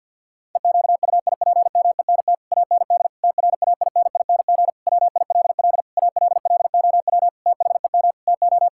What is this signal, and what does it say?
Morse code, heard as a keyed tone, about 50 wpm, 700 Hz. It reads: EQUIPMENT AND TRAINING WILL ALLOW THEM TO
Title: Sentences from Top 1000 Words - Review Part 5 - 50wpm